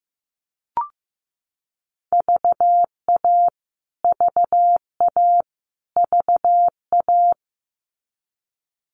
Morse code, heard as a keyed tone, about 15 wpm, 700 Hz. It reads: VA VA VA
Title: US States Repeated 3x 15wpm